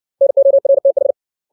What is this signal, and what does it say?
Morse code, heard as a keyed tone, about 60 wpm, 550 Hz. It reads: NORTH
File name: 632 words / 632 Words 60WPM